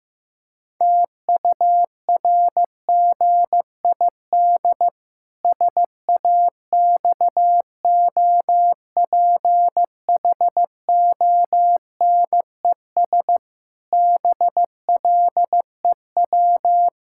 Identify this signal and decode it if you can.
Morse code, heard as a keyed tone, about 15 wpm, 700 Hz. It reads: TURGID SAXOPHONES BLEW
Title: Pangram Challenge v2 - 15wpm